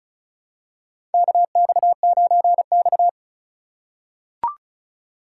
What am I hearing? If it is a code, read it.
Morse code, heard as a keyed tone, about 36 wpm, 700 Hz. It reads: KX9X
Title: USA Call Signs - 4 Characters - 4x Character Spacing - 35wpm